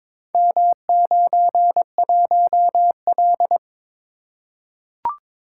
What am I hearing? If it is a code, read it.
Morse code, heard as a keyed tone, about 22 wpm, 700 Hz. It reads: M91L
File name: practice / 22wpm / Four Letter-Number Combinations - 22wpm